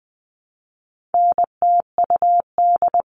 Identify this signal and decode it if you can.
Morse code, heard as a keyed tone, about 20 wpm, 700 Hz. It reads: NTUD
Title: All Cut Numbers - 4-Digit 20wpm